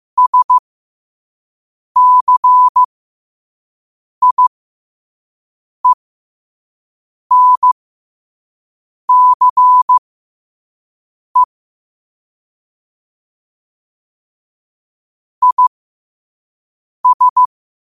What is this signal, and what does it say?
Morse code, heard as a keyed tone, about 15 wpm, spaced as for 5 wpm, 1000 Hz. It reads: SCIENCE IS
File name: qotdfiles / 5wpm